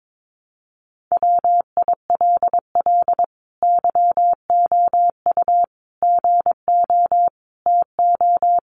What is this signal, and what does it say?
Morse code, heard as a keyed tone, about 22 wpm, 700 Hz. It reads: WILL YOU GO TO
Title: Sentences from Top 1500 Words 22wpm